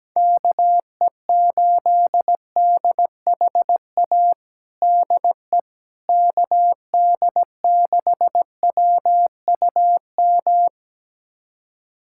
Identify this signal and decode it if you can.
Morse code, heard as a keyed tone, about 17 wpm, 700 Hz. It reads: KE8DHA DE KD6WUM
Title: USA Callsign DE Callsign - 6 Characters - 17wpm